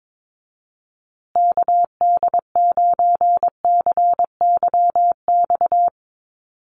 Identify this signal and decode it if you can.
Morse code, heard as a keyed tone, about 22 wpm, 700 Hz. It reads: KD9CYX
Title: USA Call Signs Repeated 3x 22wpm